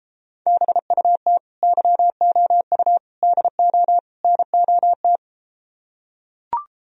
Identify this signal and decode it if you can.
Morse code, heard as a keyed tone, about 33 wpm, 700 Hz. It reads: BUT YOU DO NOT